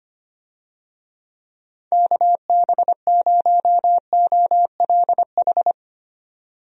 Morse code, heard as a keyed tone, about 25 wpm, 700 Hz. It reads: KB0OLH
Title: USA Call Signs - 6 Characters - 25wpm